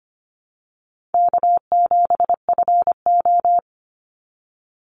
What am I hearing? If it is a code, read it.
Morse code, heard as a keyed tone, about 25 wpm, 700 Hz. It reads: K7FO
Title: NA SOTA Chaser Callsigns 25wpm